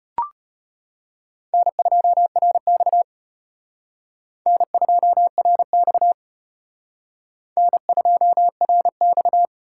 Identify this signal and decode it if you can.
Morse code, heard as a keyed tone, about 31 wpm, 700 Hz. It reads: N2RX N2RX N2RX